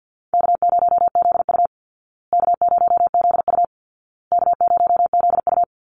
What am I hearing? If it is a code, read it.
Morse code, heard as a keyed tone, about 48 wpm, 700 Hz. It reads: K0ZU K0ZU K0ZU
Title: Easy Call Signs Repeated 3x - 50wpm